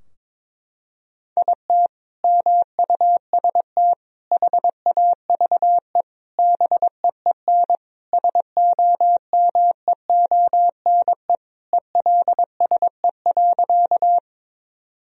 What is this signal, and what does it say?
Morse code, heard as a keyed tone, about 22 wpm, 700 Hz. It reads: IT MUST HAVE BEEN SOMEONE ELSE.
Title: Sentences from Top 1000 Words - Review Part 4 - 22wpm